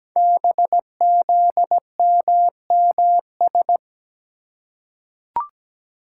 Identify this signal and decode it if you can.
Morse code, heard as a keyed tone, about 17 wpm, 700 Hz. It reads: BZMMS
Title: Code Groups - 5 Letters - 17wpm